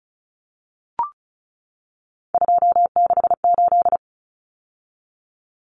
Morse code, heard as a keyed tone, about 34 wpm, 700 Hz. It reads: 268
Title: Common Cut Numbers - 3-Digit 35wpm